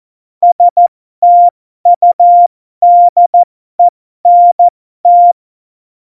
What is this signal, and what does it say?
Morse code, heard as a keyed tone, about 14 wpm, spaced as for 12 wpm, 700 Hz. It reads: STUDENT
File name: Morsekurs / Witze